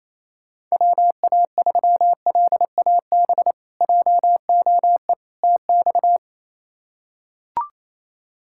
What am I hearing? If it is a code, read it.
Morse code, heard as a keyed tone, about 28 wpm, 700 Hz. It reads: WA3LAB JOE TX
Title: K1USN SST QSO Practice - 28wpm